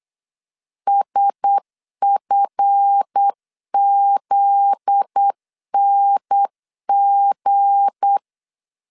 Morse code, heard as a keyed tone, about 8 wpm, 800 Hz. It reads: SFZNG
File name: LEZIONI / 8 YJ45 / 8 YJ4 8wpm 3246